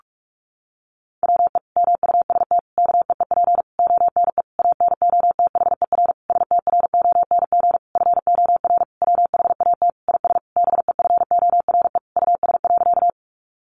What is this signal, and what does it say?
Morse code, heard as a keyed tone, about 44 wpm, 700 Hz. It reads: WE MUST KEEP ONE ANOTHER STRONG FOR WHAT IS BEFORE US.
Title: Sentences from Top 600 Words 45wpm